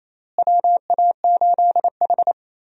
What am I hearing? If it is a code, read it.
Morse code, heard as a keyed tone, about 28 wpm, 700 Hz. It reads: WA8H
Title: USA Call Signs - 4 Characters - Repeated 3x - 28wpm